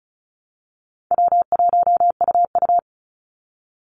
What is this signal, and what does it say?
Morse code, heard as a keyed tone, about 34 wpm, 700 Hz. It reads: W1UU
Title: Easy Call Signs Repeated 2x - 35wpm